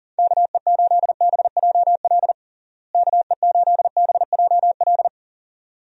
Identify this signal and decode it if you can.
Morse code, heard as a keyed tone, about 40 wpm, 700 Hz. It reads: KE8BJL KE8BJL
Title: USA Call Signs - 6 Characters - Repeated 2x - 40wpm